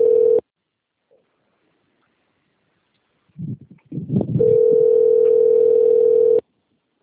Ringback tone (North America), 440 + 480 Hz.
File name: out- - - - - .14005